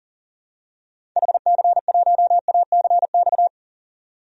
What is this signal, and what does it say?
Morse code, heard as a keyed tone, about 40 wpm, 700 Hz. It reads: HC1ACX
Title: DXCC-100 Call Signs - 6 Characters - 40wpm